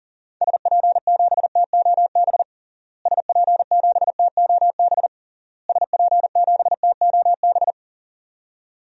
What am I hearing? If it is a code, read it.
Morse code, heard as a keyed tone, about 37 wpm, 700 Hz. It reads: SP7TOB SP7TOB SP7TOB
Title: DXCC-100 Call Signs - 6 Characters - Repeated 3x - 40wpm